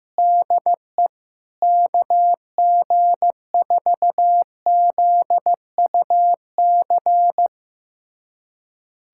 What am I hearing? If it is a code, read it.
Morse code, heard as a keyed tone, about 15 wpm, 700 Hz. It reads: DE KG4ZUC
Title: USA Callsign DE Callsign - 6 Characters - 15wpm